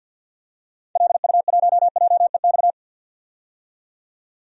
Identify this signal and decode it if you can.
Morse code, heard as a keyed tone, about 47 wpm, 700 Hz. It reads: LU1JEX
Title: DXCC-100 Call Signs - 6 Characters - 50wpm